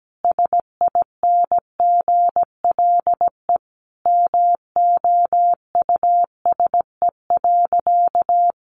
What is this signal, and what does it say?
Morse code, heard as a keyed tone, about 17 wpm, 700 Hz. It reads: SINGLE MOUSE.
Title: A Sound Of Thunder 17wpm